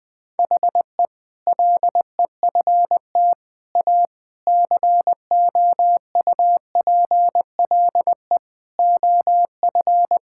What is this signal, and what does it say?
Morse code, heard as a keyed tone, about 20 wpm, 700 Hz. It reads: HE LEFT A COUPLE OF
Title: Sentences with 4-Word Phrases - 20wpm